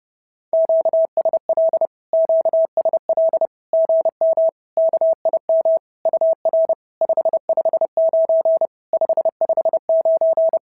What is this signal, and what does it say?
Morse code, heard as a keyed tone, about 30 wpm, 650 Hz. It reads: QSL QSL GM KIM UR 559 559
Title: SOTA QSO Practice 30wpm